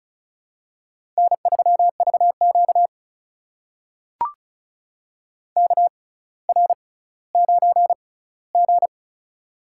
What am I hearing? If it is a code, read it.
Morse code, heard as a keyed tone, about 36 wpm, spaced as for 18 wpm, 700 Hz. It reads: N3VQ KR9G
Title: USA Call Signs - 4 Characters - 6x Character Spacing - 35wpm